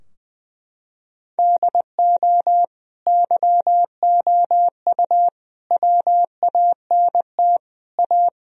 Morse code, heard as a keyed tone, about 20 wpm, 700 Hz. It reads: DO YOU WANT A
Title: Sentences from Top 1000 Words - Review Part 2 - 20wpm